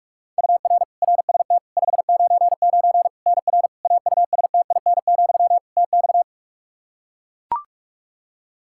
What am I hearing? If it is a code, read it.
Morse code, heard as a keyed tone, about 41 wpm, 700 Hz. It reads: UR RST 599 NR AUSTIN, TX
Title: Practice QSO - Exchange 1 - Beginning - 45wpm